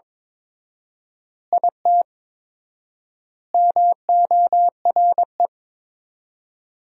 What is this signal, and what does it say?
Morse code, heard as a keyed tone, about 22 wpm, 700 Hz. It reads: IT MORE
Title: 4x Word-Spacing - Sentences from Top 400 Words - 22wpm